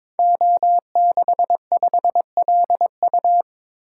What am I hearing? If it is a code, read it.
Morse code, heard as a keyed tone, about 22 wpm, 700 Hz. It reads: O65LU
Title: Code Groups - 5 Letter:Numbers - 22wpm